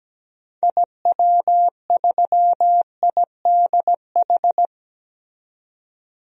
Morse code, heard as a keyed tone, about 17 wpm, 700 Hz. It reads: IW3IDH
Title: DXCC-100 Call Signs - 6 Characters - 17wpm